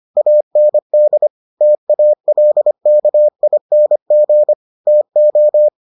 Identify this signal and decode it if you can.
Morse code, heard as a keyed tone, about 25 wpm, 600 Hz. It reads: AND TALKING TO